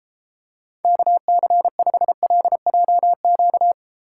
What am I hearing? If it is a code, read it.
Morse code, heard as a keyed tone, about 32 wpm, 700 Hz. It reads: KC5LJQ